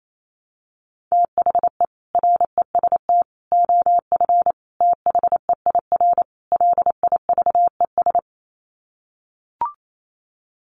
Morse code, heard as a keyed tone, about 28 wpm, 700 Hz. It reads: THE REST OF THEIR LIVES
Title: Sets of 5 Words Encore 28wpm